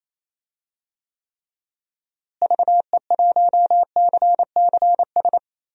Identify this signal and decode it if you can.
Morse code, heard as a keyed tone, about 28 wpm, 700 Hz. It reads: VE1CCS
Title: DXCC-100 Call Signs - 6 Characters - Repeated 2x - 28wpm